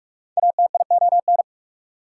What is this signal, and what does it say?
Morse code, heard as a keyed tone, about 44 wpm, 690 Hz. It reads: ATION
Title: Prefixes and Suffixes 45wpm